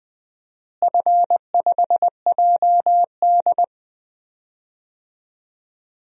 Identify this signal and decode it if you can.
Morse code, heard as a keyed tone, about 21 wpm, 700 Hz. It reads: F5JD
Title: DXCC-100 Call Signs - 4 Characters - Repeated 3x - 20wpm